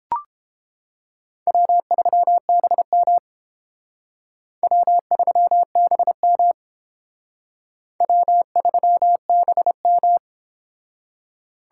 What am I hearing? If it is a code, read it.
Morse code, heard as a keyed tone, about 31 wpm, 700 Hz. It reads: W3BM W3BM W3BM